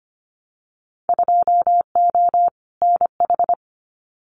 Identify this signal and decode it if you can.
Morse code, heard as a keyed tone, about 25 wpm, 700 Hz. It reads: 2O NH
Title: Winter Field Day Exchanges 25wpm